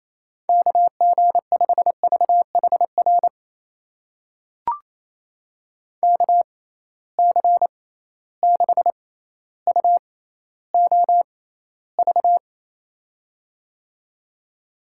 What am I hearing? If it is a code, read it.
Morse code, heard as a keyed tone, about 28 wpm, spaced as for 14 wpm, 700 Hz. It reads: KG5VHR KC6UOV
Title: USA Call Signs - 6 Characters - 6x Character Spacing - 28wpm